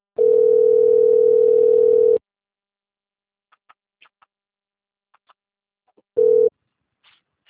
Ringback tone (North America), 440 + 480 Hz.
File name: rg-1600-unknown- - - .50427